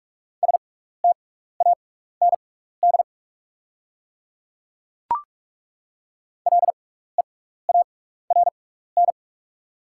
Morse code, heard as a keyed tone, about 48 wpm, spaced as for 16 wpm, 690 Hz. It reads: STAND LEARN
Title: 6x Character Spacing - Top 300 Words - 45wpm